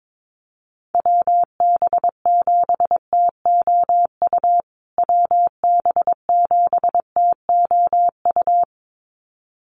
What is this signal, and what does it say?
Morse code, heard as a keyed tone, about 22 wpm, 700 Hz. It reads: WB7TOU WB7TOU
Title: USA Call Signs Repeated 2x 22wpm